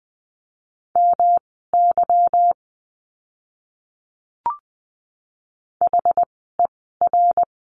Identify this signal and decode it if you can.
Morse code, heard as a keyed tone, about 20 wpm, spaced as for 14 wpm, 700 Hz. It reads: MY HER
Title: 206 Lesson 33f - Letter j - Words 20wpm